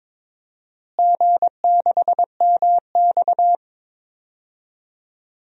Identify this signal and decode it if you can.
Morse code, heard as a keyed tone, about 22 wpm, 700 Hz. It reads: G6MX
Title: DXCC-100 Call Signs - 4 Characters - Repeated 3x - 22wpm